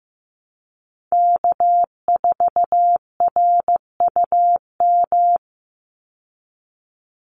Morse code, heard as a keyed tone, about 15 wpm, 700 Hz. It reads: K4RUM
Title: Call Signs Repeated 3x - 15wpm